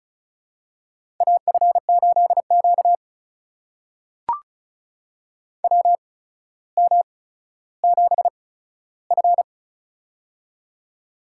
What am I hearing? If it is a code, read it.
Morse code, heard as a keyed tone, about 36 wpm, spaced as for 15 wpm, 700 Hz. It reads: AF8Q WM7F
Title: USA Call Signs - 4 Characters - 8x Character Spacing - 35wpm